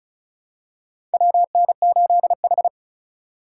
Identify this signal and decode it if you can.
Morse code, heard as a keyed tone, about 34 wpm, 700 Hz. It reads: WN8H
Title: USA Call Signs - 3x Character Spacing - 35wpm